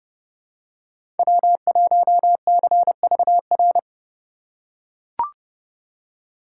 Morse code, heard as a keyed tone, about 30 wpm, 700 Hz. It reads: W1CVR
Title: USA Call Signs - 5 Characters - 4x Character Spacing - 30wpm